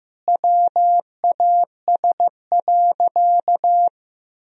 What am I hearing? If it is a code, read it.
Morse code, heard as a keyed tone, about 15 wpm, 700 Hz. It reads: WAS.
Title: Sentences from Top 1000 Words - Review Part 2 - 15wpm